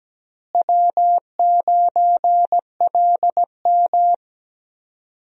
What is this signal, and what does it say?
Morse code, heard as a keyed tone, about 17 wpm, 700 Hz. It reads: W9LM
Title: USA Call Signs - 4 Characters - Repeated 2x - 17wpm